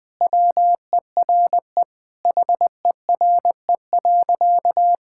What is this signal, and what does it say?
Morse code, heard as a keyed tone, about 20 wpm, 700 Hz. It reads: WERE HERE.
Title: Sentences with 2-Word Phrases - 20wpm